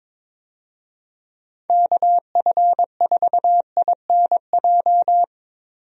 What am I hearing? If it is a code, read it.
Morse code, heard as a keyed tone, about 22 wpm, 700 Hz. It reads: KF4INJ
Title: USA Call Signs - 6 Characters - Repeated 3x - 22wpm